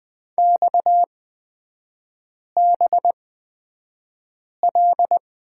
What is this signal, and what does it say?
Morse code, heard as a keyed tone, about 20 wpm, spaced as for 5 wpm, 690 Hz. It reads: XBL
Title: 229 Lesson 37e - Letter x - Three Characters 20wpm